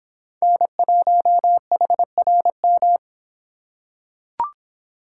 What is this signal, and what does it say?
Morse code, heard as a keyed tone, about 26 wpm, 700 Hz. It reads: N1HRM